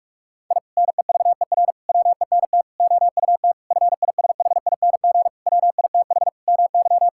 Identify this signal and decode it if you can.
Morse code, heard as a keyed tone, about 41 wpm, 700 Hz. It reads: I NEVER WENT OUT FISHING WITH MY
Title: Sentences from Top 1000 Words - Review Part 5 - 45wpm